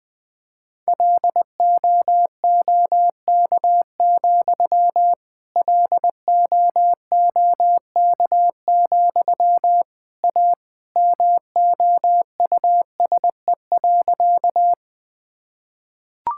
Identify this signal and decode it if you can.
Morse code, heard as a keyed tone, about 21 wpm, 700 Hz. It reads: LOOK, LOOK, A MOUSE.
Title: Sentences from Top 1000 Words - Review Part 5 - 20wpm